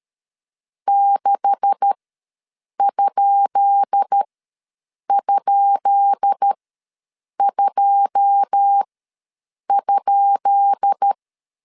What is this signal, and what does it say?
Morse code, heard as a keyed tone, about 13 wpm, spaced as for 7 wpm, 800 Hz. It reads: 6??2?
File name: LEZIONI / 10 SIGNS / 10B INT ws 12wpm 3527